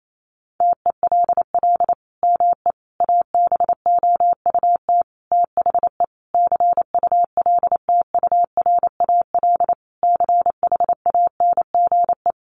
Morse code, heard as a keyed tone, about 28 wpm, 700 Hz. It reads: TELL ME ABOUT THE CULTURAL CHANGE
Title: Sentences from Top 1500 Words 28wpm